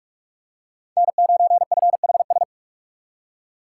Morse code, heard as a keyed tone, about 42 wpm, 700 Hz. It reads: N9FHS
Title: USA Call Signs - 2x Character Spacing - 45wpm